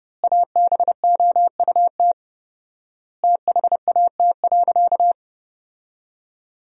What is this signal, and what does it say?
Morse code, heard as a keyed tone, about 30 wpm, 700 Hz. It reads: ABOUT THAT.
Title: 4x Word-Spacing - Sentences from Top 200 Words - 30wpm